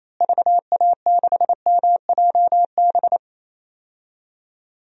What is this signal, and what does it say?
Morse code, heard as a keyed tone, about 28 wpm, 700 Hz. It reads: VA6MJB
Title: DXCC-100 Call Signs - 6 Characters - Repeated 2x - 28wpm